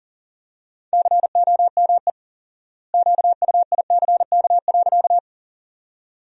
Morse code, heard as a keyed tone, about 40 wpm, 700 Hz. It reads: COME QUICK.
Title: 4x Word-Spacing - Sentences from Top 500 Words - 40wpm